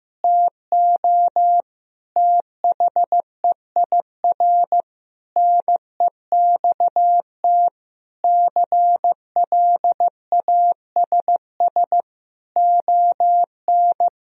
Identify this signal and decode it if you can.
Morse code, heard as a keyed tone, about 15 wpm, 700 Hz. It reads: TO THEIR NEXT CLASS ON
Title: Sentences from Top 1000 Words - Review Part 2 - 15wpm